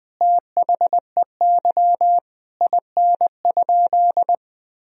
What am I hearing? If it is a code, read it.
Morse code, heard as a keyed tone, about 20 wpm, 700 Hz. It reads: THEY IN?
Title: Sentences from Top 1000 Words - Review Part 2 - 20wpm